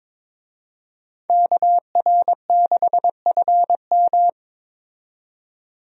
Morse code, heard as a keyed tone, about 22 wpm, 700 Hz. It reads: KR6FM
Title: USA Call Signs - 5 Characters - 22wpm